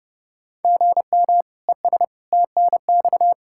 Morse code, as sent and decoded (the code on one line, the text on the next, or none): --. -- / . ... / - -. -..-
GM ES TNX